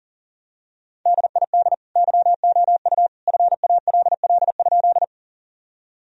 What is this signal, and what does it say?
Morse code, heard as a keyed tone, about 37 wpm, 700 Hz. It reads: DID YOU FALL?
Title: Sentences from Top 1000 Words - Review Part 2 - 40wpm